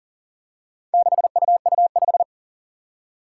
Morse code, heard as a keyed tone, about 40 wpm, 700 Hz. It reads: 6UU5
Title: Four Challenging Letter-Number Combinations 40wpm